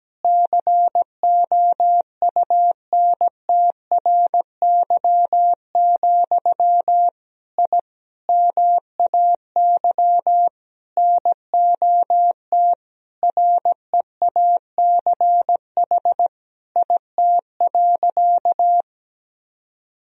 Morse code, heard as a keyed tone, about 17 wpm, 700 Hz. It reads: COUNTRY, I MAY NOT REACH IT.
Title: Sentences from Top 1000 Words - Review Part 2 - 17wpm